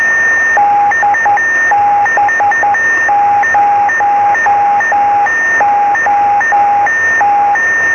Morse code, heard as a keyed tone, about 10 wpm, 820 Hz, with noise. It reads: DB0OT
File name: sm7lcb / db0ot-b 23cm